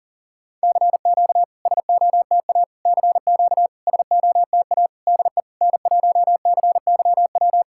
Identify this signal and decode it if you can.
Morse code, heard as a keyed tone, about 40 wpm, 690 Hz. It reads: CQ SOTA CQ SOTA DE N1CYW
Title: SOTA QSO Practice 40wpm